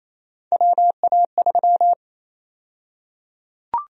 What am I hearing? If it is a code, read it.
Morse code, heard as a keyed tone, about 28 wpm, 700 Hz. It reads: WA3
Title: USA Call Sign Prefixes - 28wpm